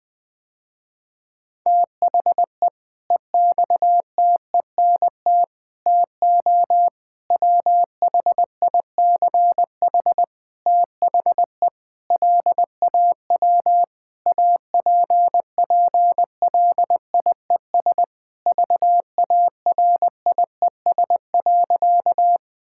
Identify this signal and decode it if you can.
Morse code, heard as a keyed tone, about 20 wpm, 700 Hz. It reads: THE EXTENT TO WHICH THE LAW APPLIES VARIES.
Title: Sentences with 5-Word Phrases - Spoken Beginning - 20wpm